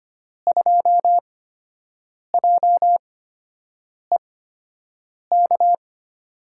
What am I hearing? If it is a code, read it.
Morse code, heard as a keyed tone, about 25 wpm, spaced as for 7 wpm, 700 Hz. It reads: 2JEK
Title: USA Call Signs - 5 Characters - 8x Character Spacing - 25wpm